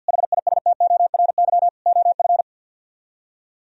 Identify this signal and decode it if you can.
Morse code, heard as a keyed tone, about 62 wpm, 700 Hz. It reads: HISTORY OF
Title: Speed-Racing - Sets of 3 Words 50wpm